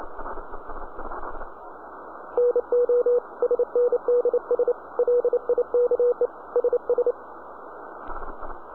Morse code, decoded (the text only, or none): NO SNDS LIKE SS